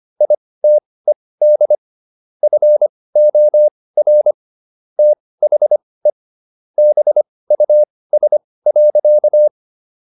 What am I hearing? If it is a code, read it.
Morse code, heard as a keyed tone, about 25 wpm, spaced as for 18 wpm, 600 Hz. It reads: ITED FOR THE BUS.